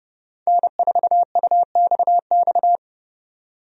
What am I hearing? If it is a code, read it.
Morse code, heard as a keyed tone, about 30 wpm, 700 Hz. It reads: N4UXX
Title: USA Call Signs - 5 Characters - 30wpm